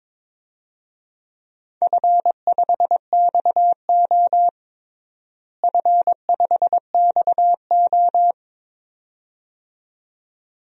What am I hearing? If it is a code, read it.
Morse code, heard as a keyed tone, about 22 wpm, 700 Hz. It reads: F5XO F5XO
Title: DXCC-100 Call Signs - 4 Characters - Repeated 2x - 22wpm